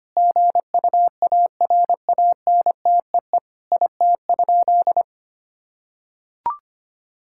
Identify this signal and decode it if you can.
Morse code, heard as a keyed tone, about 25 wpm, 700 Hz. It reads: GUARANTEE IT?
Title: Sentences from Top 1600 Words 25wpm